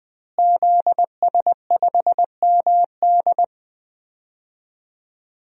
Morse code, heard as a keyed tone, about 20 wpm, 700 Hz. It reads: ZS5MD
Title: DXCC-100 Call Signs - 5 Characters - Repeated 2x - 20wpm